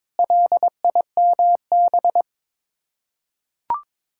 Morse code, heard as a keyed tone, about 22 wpm, 700 Hz. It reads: LIMB
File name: practice / 22wpm / Sets of 8 Words - 22wpm